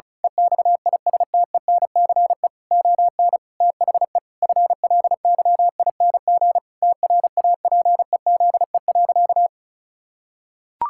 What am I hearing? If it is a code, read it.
Morse code, heard as a keyed tone, about 36 wpm, 700 Hz. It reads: EXISTENCE ON THE FLYING TRAPEZE.
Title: Pangram Challenge v2 - 35wpm